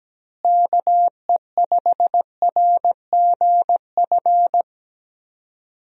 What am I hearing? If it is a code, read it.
Morse code, heard as a keyed tone, about 17 wpm, 700 Hz. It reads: KE5RGF
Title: USA Call Signs - 6 Characters - Repeated 2x - 17wpm